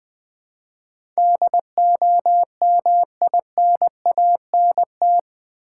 Morse code, heard as a keyed tone, about 20 wpm, 700 Hz. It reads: DOMINANT
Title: Build A Word - 4 Letters At A Time - 8-Letter Words - 20wpm